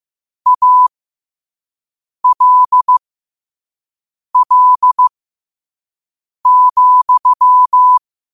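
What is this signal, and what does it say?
Morse code, heard as a keyed tone, about 15 wpm, spaced as for 5 wpm, 1000 Hz. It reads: ALL,
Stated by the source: Morse code sent at 5 words per minute.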